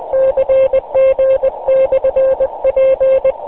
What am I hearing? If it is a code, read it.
Morse code, heard as a keyed tone, about 20 wpm, 530 Hz. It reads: CG/P
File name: fs-w6jkv-qso